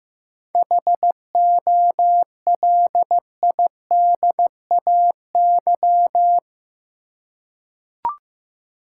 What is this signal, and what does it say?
Morse code, heard as a keyed tone, about 15 wpm, 700 Hz. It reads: HOLIDAY
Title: Seven Letter Words - 15wpm